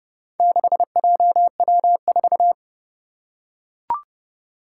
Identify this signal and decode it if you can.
Morse code, heard as a keyed tone, about 30 wpm, 700 Hz. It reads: 6JW4
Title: Four Challenging Letter-Number Combinations 30wpm